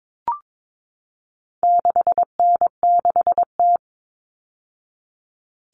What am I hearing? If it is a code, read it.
Morse code, heard as a keyed tone, about 22 wpm, 700 Hz. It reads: 6N6T
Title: All Cut Numbers - 4-Digit 22wpm